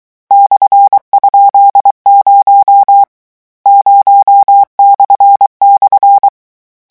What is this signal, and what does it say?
Morse code, heard as a keyed tone, about 23 wpm, 800 Hz. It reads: /?0 0//
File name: morse mp3 / lecon13-vitesse 22